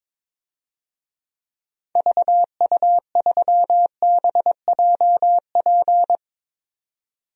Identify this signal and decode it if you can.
Morse code, heard as a keyed tone, about 21 wpm, 700 Hz. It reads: VU3BJP
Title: DXCC-100 Call Signs - 6 Characters - 22wpm